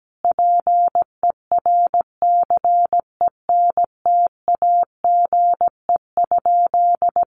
Morse code, heard as a keyed tone, about 17 wpm, 700 Hz. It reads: PERCENTAGE?
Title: Sentences from Top 1300 Words 17wpm